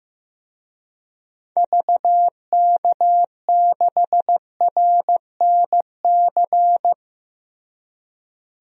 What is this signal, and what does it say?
Morse code, heard as a keyed tone, about 15 wpm, 700 Hz. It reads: VK6RNC
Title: DXCC-100 Call Signs - 6 Characters - Repeated 3x - 15wpm